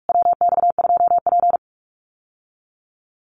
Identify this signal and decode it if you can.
Morse code, heard as a keyed tone, about 44 wpm, 700 Hz. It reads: WX2P
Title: USA Call Signs Repeated 3x 45wpm